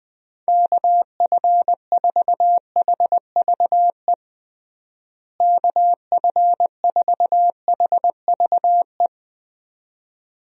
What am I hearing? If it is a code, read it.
Morse code, heard as a keyed tone, about 21 wpm, 700 Hz. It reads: KF4HVE KF4HVE
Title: USA Call Signs - 6 Characters - Repeated 2x - 20wpm